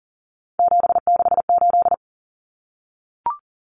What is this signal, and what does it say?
Morse code, heard as a keyed tone, about 40 wpm, 700 Hz. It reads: 768
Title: Common Cut Numbers - 3-Digit 40wpm